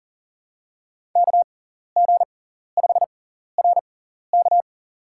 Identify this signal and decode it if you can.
Morse code, heard as a keyed tone, about 40 wpm, spaced as for 14 wpm, 700 Hz. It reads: KG5RK
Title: USA Call Signs - 5 Characters - 6x Character Spacing - 40wpm